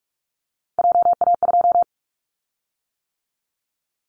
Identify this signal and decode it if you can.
Morse code, heard as a keyed tone, about 46 wpm, 700 Hz. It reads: JA2
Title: Callsign Prefixes for Top 50 DXCC - 45wpm